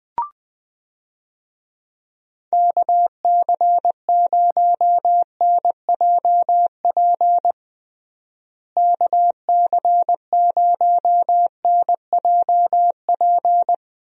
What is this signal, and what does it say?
Morse code, heard as a keyed tone, about 21 wpm, 700 Hz. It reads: KC0NJP KC0NJP
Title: USA Call Signs - 6 Characters - Repeated 2x - 20wpm